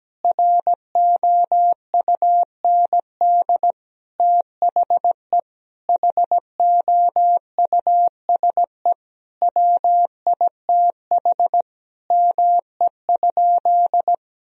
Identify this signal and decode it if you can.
Morse code, heard as a keyed tone, about 17 wpm, 700 Hz. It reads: ROUND THE HOUSE WITH ME?
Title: Sentences from Top 1000 Words - Review Part 1 - 17wpm